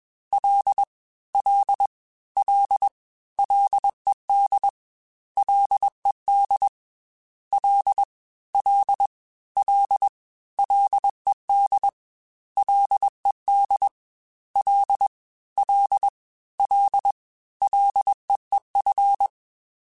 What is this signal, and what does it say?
Morse code, heard as a keyed tone, about 21 wpm, 800 Hz. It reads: L L L LED LED L L L LED LED L L L LEEF
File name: Ls / Single L m3s / Single L @20